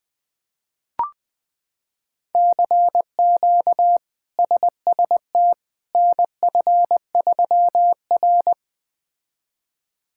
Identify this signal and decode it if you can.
Morse code, heard as a keyed tone, about 20 wpm, 700 Hz. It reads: CQ SST NF3R
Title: K1USN SST QSO Practice - 20wpm